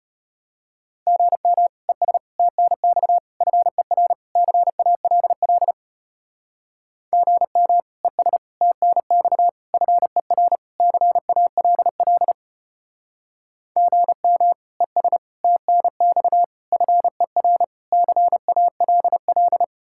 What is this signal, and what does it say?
Morse code, heard as a keyed tone, about 32 wpm, 700 Hz. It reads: GM ES TNX FER CALL GM ES TNX FER CALL GM ES TNX FER CALL